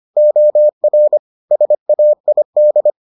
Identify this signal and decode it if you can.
Morse code, heard as a keyed tone, about 25 wpm, 600 Hz. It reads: OR SAID